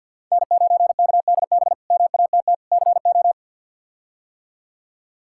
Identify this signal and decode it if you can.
Morse code, heard as a keyed tone, about 46 wpm, 700 Hz. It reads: N9KDB MATT CO
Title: K1USN SST QSO Practice - 50wpm